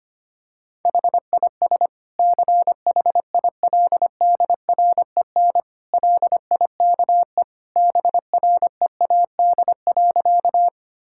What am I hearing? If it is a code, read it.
Morse code, heard as a keyed tone, about 25 wpm, 700 Hz. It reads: HIS CHILDREN LIKE BREAD.
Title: Sentences from Top 1600 Words 25wpm